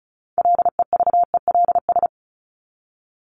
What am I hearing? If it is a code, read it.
Morse code, heard as a keyed tone, about 34 wpm, 700 Hz. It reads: LEVELS
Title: ES:S Suffix - Top 100 Words with Base Words 35wpm